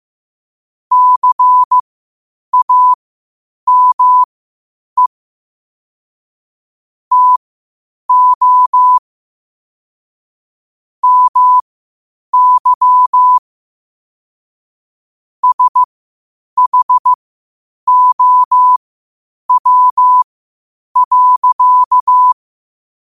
Morse code, heard as a keyed tone, about 15 wpm, spaced as for 8 wpm, 1000 Hz. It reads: CAME TO MY SHOW.
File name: 8wpm